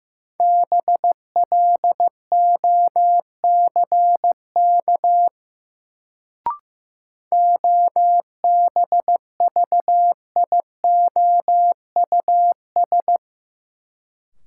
Morse code, heard as a keyed tone, about 15 wpm, 700 Hz. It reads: BLOCK OBVIOUS
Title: Sentences from Top 1600 Words - New Word Drill Alt - 15wpm